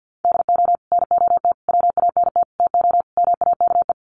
Morse code, heard as a keyed tone, about 50 wpm, 700 Hz. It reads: DO NOT WANT TO MAKE
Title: Sets of 5 Words Encore 50wpm